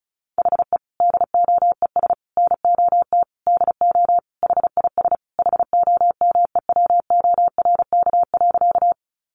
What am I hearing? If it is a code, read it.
Morse code, heard as a keyed tone, about 34 wpm, 700 Hz. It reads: HE DOES NOT DO HIS HOMEWORK.
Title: Additional Easy Phrases 35wpm